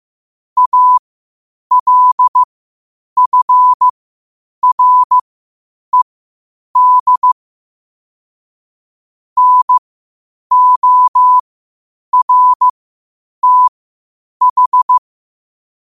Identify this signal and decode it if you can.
Morse code, heard as a keyed tone, about 15 wpm, spaced as for 8 wpm, 1000 Hz. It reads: ALFRED NORTH
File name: qotdfiles / 8wpm